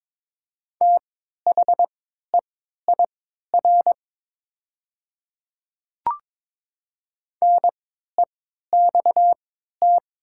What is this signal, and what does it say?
Morse code, heard as a keyed tone, about 22 wpm, spaced as for 13 wpm, 700 Hz. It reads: THEIR NEXT
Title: 3x Character Spacing - Top 300 Words - 22wpm